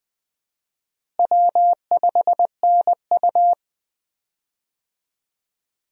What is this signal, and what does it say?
Morse code, heard as a keyed tone, about 20 wpm, 700 Hz. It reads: W5NU
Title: USA Call Signs - 4 Characters - Repeated 2x - 20wpm